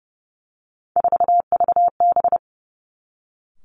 Morse code, heard as a keyed tone, about 30 wpm, 700 Hz. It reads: 4VB
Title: Three Challenging Letter-Number Combinations 30wpm